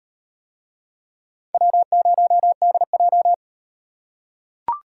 Morse code, heard as a keyed tone, about 38 wpm, 700 Hz. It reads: W0DJ